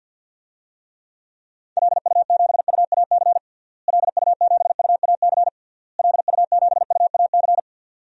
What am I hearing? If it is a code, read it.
Morse code, heard as a keyed tone, about 52 wpm, 700 Hz. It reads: LU7UAC LU7UAC LU7UAC
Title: DXCC-100 Call Signs - 6 Characters - Repeated 3x - 50wpm